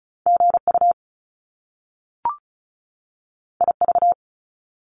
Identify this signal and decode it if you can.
Morse code, heard as a keyed tone, about 36 wpm, 700 Hz. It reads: GU IV
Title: Top Two Letter Combinations in Words 35wpm